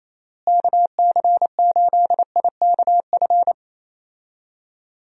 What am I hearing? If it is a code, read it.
Morse code, heard as a keyed tone, about 28 wpm, 700 Hz. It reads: KC8IKF
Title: USA Call Signs - 6 Characters - Repeated 3x - 28wpm